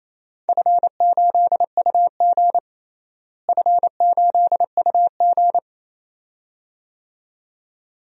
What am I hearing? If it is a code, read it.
Morse code, heard as a keyed tone, about 27 wpm, 700 Hz. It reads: F8UG F8UG
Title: DXCC-100 Call Signs - 4 Characters - Repeated 2x - 28wpm